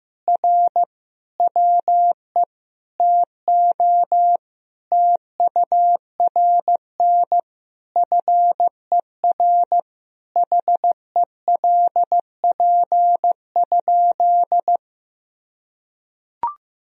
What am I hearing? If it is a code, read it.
Morse code, heard as a keyed tone, about 16 wpm, 700 Hz. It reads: R WE TO TURN FER HELP?
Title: Sentences from Top 200 Words and Abbreviations 15wpm